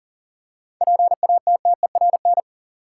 Morse code, heard as a keyed tone, about 40 wpm, 700 Hz. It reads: PATTERN
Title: Seven Letter Words - 40wpm